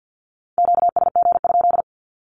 Morse code, heard as a keyed tone, about 50 wpm, 700 Hz. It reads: QSG?
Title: Q-Codes with Spoken Meaning - 50wpm